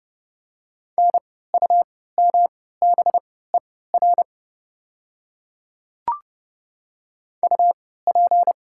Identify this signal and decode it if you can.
Morse code, heard as a keyed tone, about 29 wpm, spaced as for 17 wpm, 700 Hz. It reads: NUMBER UP
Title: 3x Character Spacing - Top 100 Words - 30wpm